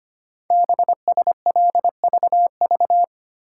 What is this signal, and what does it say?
Morse code, heard as a keyed tone, about 25 wpm, 700 Hz. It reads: BSLVV
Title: Code Groups - 5 Letters - 25wpm